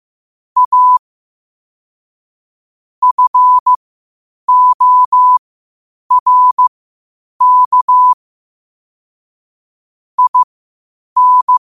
Morse code, heard as a keyed tone, about 15 wpm, spaced as for 8 wpm, 1000 Hz. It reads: A FORK IN
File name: qotdfiles / 8wpm